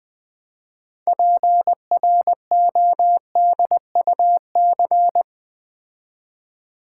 Morse code, heard as a keyed tone, about 20 wpm, 700 Hz. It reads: PRODUC
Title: Build A Word - 6 Letters At A Time - 12-Letter Words - 20wpm